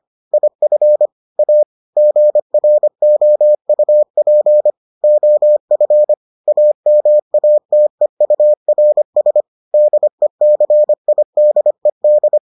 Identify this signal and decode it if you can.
Morse code, heard as a keyed tone, about 25 wpm, 600 Hz. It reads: IF A GROUP OF AMATEURS DECIDED